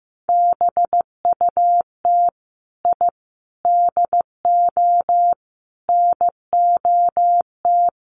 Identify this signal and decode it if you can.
Morse code, heard as a keyed tone, about 15 wpm, 700 Hz. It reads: BUT I DO NOT
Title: Sets of 5 Words 15wpm